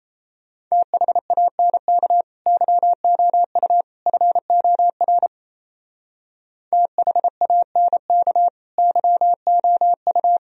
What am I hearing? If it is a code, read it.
Morse code, heard as a keyed tone, about 36 wpm, 700 Hz. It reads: THANK YOU FOR THANK YOU